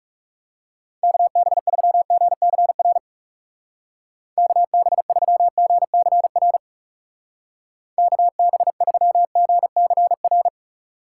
Morse code, heard as a keyed tone, about 38 wpm, 700 Hz. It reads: KB3GCR KB3GCR KB3GCR
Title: Speed Racing - USA 2x3 Call Signs 30wpm